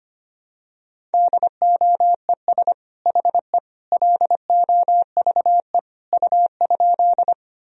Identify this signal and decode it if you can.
Morse code, heard as a keyed tone, about 25 wpm, 700 Hz. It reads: DOES HE LOVE U?
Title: Sentences from Top 400 Words and Abbreviations 25wpm